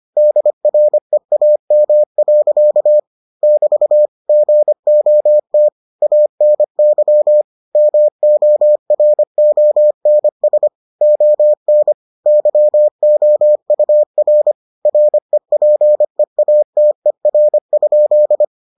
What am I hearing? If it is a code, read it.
Morse code, heard as a keyed tone, about 25 wpm, 600 Hz. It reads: DREAM. <BT> GOT ANY MORONS ON YOUR REPEATER?